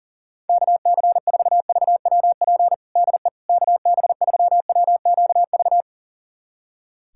Morse code, heard as a keyed tone, about 40 wpm, 700 Hz. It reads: KC4VWP DE KB3WQV
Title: USA Callsign DE Callsign 40wpm